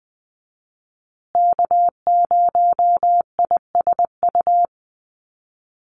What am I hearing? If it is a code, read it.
Morse code, heard as a keyed tone, about 20 wpm, 700 Hz. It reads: K0ISU
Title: USA Call Signs Repeated 2x 20wpm